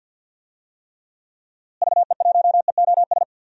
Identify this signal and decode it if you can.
Morse code, heard as a keyed tone, about 52 wpm, 700 Hz. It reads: VE1EOS
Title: DXCC-100 Call Signs - 6 Characters - Repeated 3x - 50wpm